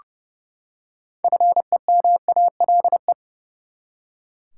Morse code, heard as a keyed tone, about 29 wpm, 700 Hz. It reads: FEMALE
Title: Six Letter Words - 30wpm